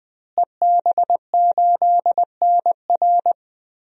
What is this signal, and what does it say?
Morse code, heard as a keyed tone, about 20 wpm, 700 Hz. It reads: EB8NR
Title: DXCC-100 Call Signs - 5 Characters - Repeated 2x - 20wpm